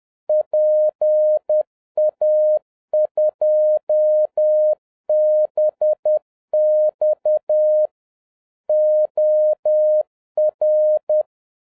Morse code, heard as a keyed tone, about 10 wpm, 610 Hz. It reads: PA2BX OR